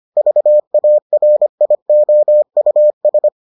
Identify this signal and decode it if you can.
Morse code, heard as a keyed tone, about 25 wpm, 600 Hz. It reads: VARIOUS